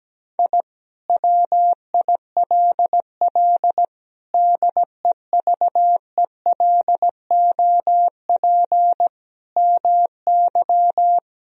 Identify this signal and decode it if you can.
Morse code, heard as a keyed tone, about 17 wpm, 700 Hz. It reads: I WILL DEVELOP MY
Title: Sentences from Top 1000 Words - Review Part 2 - 17wpm